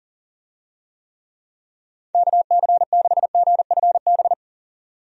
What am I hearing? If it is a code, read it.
Morse code, heard as a keyed tone, about 40 wpm, 700 Hz. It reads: KC6GFB
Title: USA Call Signs - 6 Characters - 40wpm